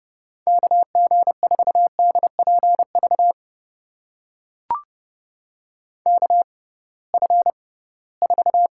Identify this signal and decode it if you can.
Morse code, heard as a keyed tone, about 30 wpm, spaced as for 19 wpm, 700 Hz. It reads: KG4DPV KF4
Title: USA Call Signs - 6 Characters - 6x Character Spacing - 30wpm